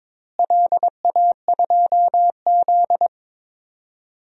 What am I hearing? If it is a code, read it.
Morse code, heard as a keyed tone, about 22 wpm, 700 Hz. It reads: LA2Z
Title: DXCC-100 Call Signs - 4 Characters - Repeated 3x - 22wpm